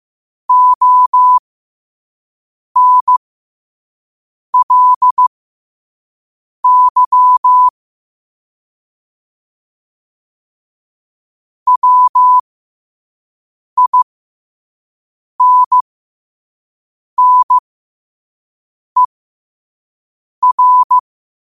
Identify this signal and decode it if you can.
Morse code, heard as a keyed tone, about 15 wpm, spaced as for 5 wpm, 1000 Hz. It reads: ONLY WINNER
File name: qotdfiles / 5wpm